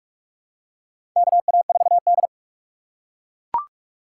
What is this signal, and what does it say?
Morse code, heard as a keyed tone, about 41 wpm, 700 Hz. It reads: KA4D
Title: USA Call Signs - 4 Characters - 4x Character Spacing - 45wpm